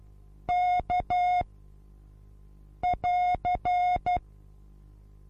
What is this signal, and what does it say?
Morse code, heard as a keyed tone, about 12 wpm, 720 Hz. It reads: K <AR>
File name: Onlinelehrgang / morsen / DJ4UF Morsen23